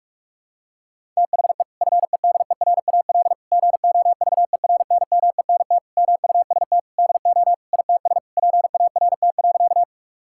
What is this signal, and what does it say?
Morse code, heard as a keyed tone, about 41 wpm, 700 Hz. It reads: THE FEDERAL GOVERNMENT MUST DO ITS PART.
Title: Sentences from Top 1000 Words - Review Part 5 - 45wpm